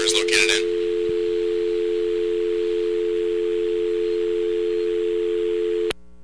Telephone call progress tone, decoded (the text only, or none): dial tone